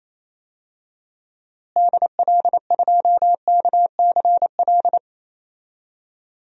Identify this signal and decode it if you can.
Morse code, heard as a keyed tone, about 28 wpm, 700 Hz. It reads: DL2KCL
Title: DXCC-100 Call Signs - 6 Characters - Repeated 3x - 28wpm